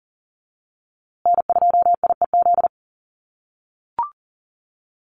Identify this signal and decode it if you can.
Morse code, heard as a keyed tone, about 40 wpm, 700 Hz. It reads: N2IEZ
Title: USA Call Signs Repeated 3x 40wpm